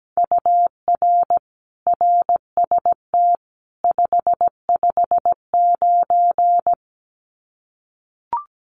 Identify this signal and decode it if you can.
Morse code, heard as a keyed tone, about 17 wpm, 700 Hz. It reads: UR RST 559
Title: Common QSO Phrases 17wpm